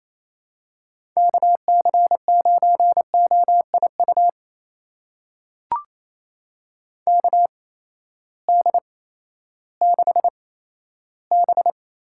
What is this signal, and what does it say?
Morse code, heard as a keyed tone, about 28 wpm, spaced as for 14 wpm, 700 Hz. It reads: KC9OIU KD6B
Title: USA Call Signs - 6 Characters - 8x Character Spacing - 28wpm